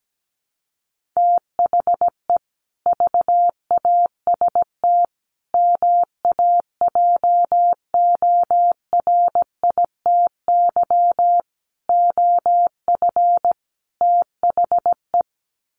Morse code, heard as a keyed tone, about 17 wpm, 700 Hz. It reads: THE VAST MAJORITY OF THE
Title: Sets of 5 Words Encore 17wpm